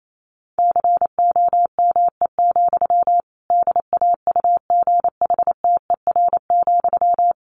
Morse code, heard as a keyed tone, about 28 wpm, 700 Hz. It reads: COME, DAUGHTER,
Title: Sentences from Top 1500 Words 28wpm